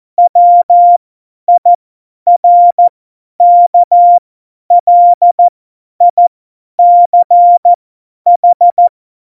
Morse code, heard as a keyed tone, about 14 wpm, spaced as for 10 wpm, 700 Hz. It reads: WIRKLICH
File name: Morsekurs / Witze / Witz 20. Oktober 2020, 50 F70